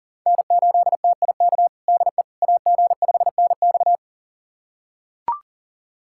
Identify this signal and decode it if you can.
Morse code, heard as a keyed tone, about 40 wpm, 700 Hz. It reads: N8TIK DE AG5NX
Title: USA Callsign DE Callsign - 5 Characters - 40wpm